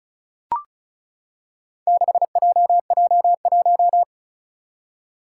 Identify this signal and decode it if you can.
Morse code, heard as a keyed tone, about 36 wpm, 700 Hz. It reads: 6JJ1
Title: Four Challenging Letter-Number Combinations 35wpm